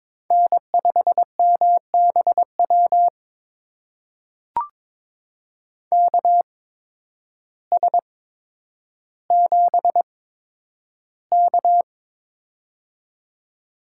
Morse code, heard as a keyed tone, about 22 wpm, spaced as for 10 wpm, 700 Hz. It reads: N5MBW KS7K
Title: USA Call Signs - 8x Character Spacing - 22wpm